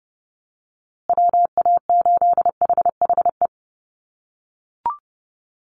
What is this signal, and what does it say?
Morse code, heard as a keyed tone, about 30 wpm, 700 Hz. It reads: WA8HHE
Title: USA Call Signs Repeated 2x 30wpm